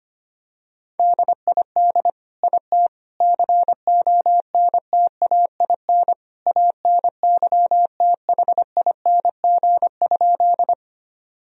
Morse code, heard as a keyed tone, about 25 wpm, 700 Hz. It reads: DID IT CONTAIN ANYTHING?
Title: Sentences from Top 1000 Words - Review Part 2 - 25wpm